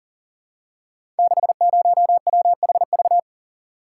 Morse code, heard as a keyed tone, about 42 wpm, 700 Hz. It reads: 60WHV
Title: Code Groups - 5 Letter:Numbers - 40wpm